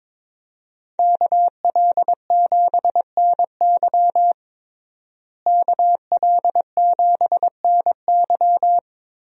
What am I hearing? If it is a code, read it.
Morse code, heard as a keyed tone, about 22 wpm, 700 Hz. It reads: KL7NY KL7NY
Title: USA Call Signs - 5 Characters - Repeated 3x - 22wpm